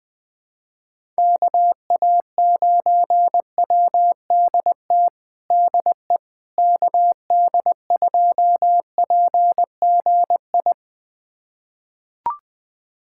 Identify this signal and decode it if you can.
Morse code, heard as a keyed tone, about 20 wpm, 700 Hz. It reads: KA9WDT DE KD2PGI
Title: USA Callsign DE Callsign - 6 Characters - 20wpm